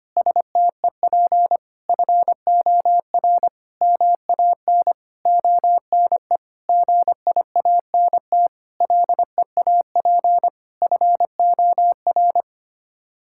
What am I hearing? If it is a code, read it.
Morse code, heard as a keyed tone, about 25 wpm, 700 Hz. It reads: STEP FOR MAN ONE GIANT LEAP FOR
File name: Sets of 9 Words - 25wpm